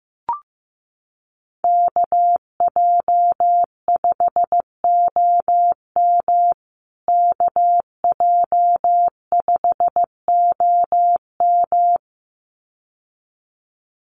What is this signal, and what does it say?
Morse code, heard as a keyed tone, about 15 wpm, 700 Hz. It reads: KJ5OM KJ5OM
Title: USA Call Signs Repeated 2x 15wpm